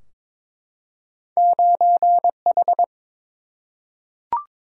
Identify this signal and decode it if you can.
Morse code, heard as a keyed tone, about 22 wpm, 700 Hz. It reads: 9 H
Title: Single-Letter Single-Number 22wpm